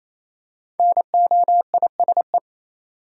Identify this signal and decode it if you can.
Morse code, heard as a keyed tone, about 28 wpm, 700 Hz. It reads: NOISE
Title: Five Letter Words - 28wpm